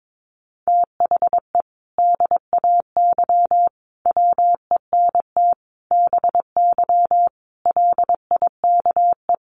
Morse code, heard as a keyed tone, about 22 wpm, 700 Hz. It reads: THE DAY WENT BY LIKE
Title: Sentences from Top 1500 Words 22wpm